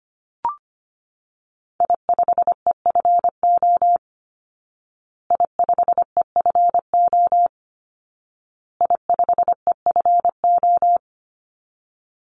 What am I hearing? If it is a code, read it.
Morse code, heard as a keyed tone, about 25 wpm, 700 Hz. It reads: I5EFO I5EFO I5EFO
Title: Call Signs Repeated 2x - 25wpm